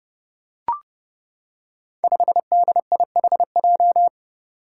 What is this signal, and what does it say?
Morse code, heard as a keyed tone, about 30 wpm, 700 Hz. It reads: 5DIHJ
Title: Five Challenging Letter-Number Combinations 30wpm